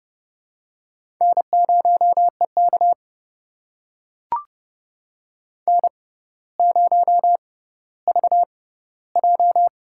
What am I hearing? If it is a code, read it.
Morse code, heard as a keyed tone, about 30 wpm, spaced as for 15 wpm, 700 Hz. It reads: N0EK N0VJ
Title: USA Call Signs - 4 Characters - 6x Character Spacing - 30wpm